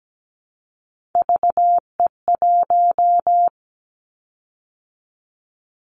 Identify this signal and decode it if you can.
Morse code, heard as a keyed tone, about 17 wpm, 700 Hz. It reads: VE1
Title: Callsign Prefixes for Top 100 DXCC - 17wpm